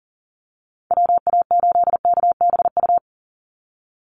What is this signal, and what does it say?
Morse code, heard as a keyed tone, about 40 wpm, 700 Hz. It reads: WA8KBU
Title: Call Signs Repeated 3x - 40wpm